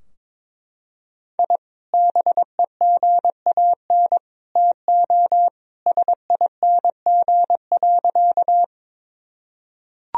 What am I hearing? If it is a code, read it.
Morse code, heard as a keyed tone, about 22 wpm, 700 Hz. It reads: I BEGAN TO SING.
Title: Sentences from Top 1000 Words - Review Part 2 - 22wpm